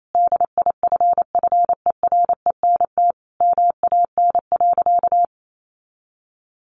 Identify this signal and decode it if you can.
Morse code, heard as a keyed tone, about 28 wpm, 700 Hz. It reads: DIFFERENT MAN.
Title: Sentences from Top 600 Words 28wpm